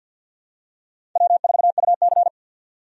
Morse code, heard as a keyed tone, about 47 wpm, 700 Hz. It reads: W4UC
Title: USA Call Signs - 4 Characters - 50wpm